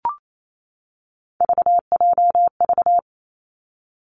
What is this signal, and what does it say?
Morse code, heard as a keyed tone, about 28 wpm, 700 Hz. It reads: VJV
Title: Three Challenging Letter-Number Combinations 28wpm